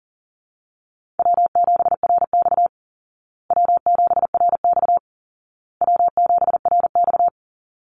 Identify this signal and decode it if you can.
Morse code, heard as a keyed tone, about 37 wpm, 700 Hz. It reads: W7RX W7RX W7RX
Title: Call Signs Repeated 2x - 40wpm